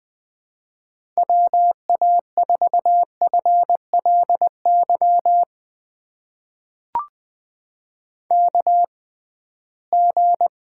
Morse code, heard as a keyed tone, about 20 wpm, spaced as for 15 wpm, 700 Hz. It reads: WA4FLY KG
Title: USA Call Signs - 6 Characters - 6x Character Spacing - 20wpm